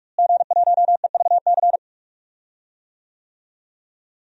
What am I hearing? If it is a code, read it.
Morse code, heard as a keyed tone, about 44 wpm, 700 Hz. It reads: G1EVC
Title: DXCC-100 Call Signs - 5 Characters - Repeated 3x - 45wpm